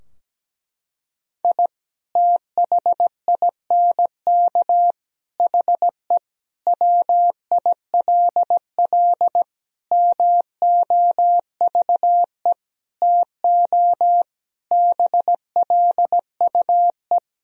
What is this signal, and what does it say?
Morse code, heard as a keyed tone, about 17 wpm, 700 Hz. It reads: I THINK HE WILL MOVE TO BLUE
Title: Sentences from Top 1000 Words - Review Part 3 - 17wpm